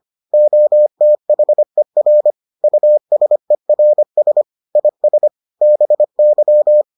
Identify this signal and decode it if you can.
Morse code, heard as a keyed tone, about 25 wpm, 600 Hz. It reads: OTHER USERS IS BY